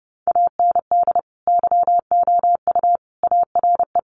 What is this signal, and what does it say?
Morse code, heard as a keyed tone, about 30 wpm, 700 Hz. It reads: AND YOU ARE
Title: Sentences from Top 400 Words 30wpm